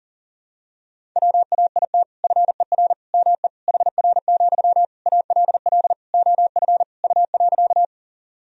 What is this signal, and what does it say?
Morse code, heard as a keyed tone, about 40 wpm, 700 Hz. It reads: WAIT FER ME HR, ALL OF U.
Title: Sentences from Top 500 Words and Abbreviations 40wpm